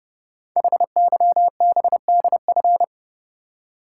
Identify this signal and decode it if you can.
Morse code, heard as a keyed tone, about 30 wpm, 700 Hz. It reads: HYBDF
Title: Code Groups - 5 Letter:Numbers - 30wpm